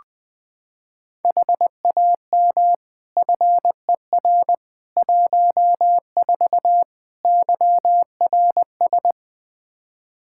Morse code, heard as a keyed tone, about 21 wpm, 700 Hz. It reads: HAM FER 14 YRS
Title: K1USN SST QSO Practice 20wpm